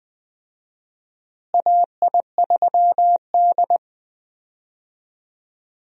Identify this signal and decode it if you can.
Morse code, heard as a keyed tone, about 20 wpm, 700 Hz. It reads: AI3D
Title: USA Call Signs - 4 Characters - Repeated 3x - 20wpm